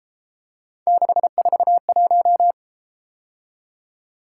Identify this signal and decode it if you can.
Morse code, heard as a keyed tone, about 33 wpm, 700 Hz. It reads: 641